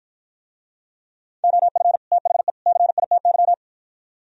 Morse code, heard as a keyed tone, about 50 wpm, 700 Hz. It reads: OF THE CITY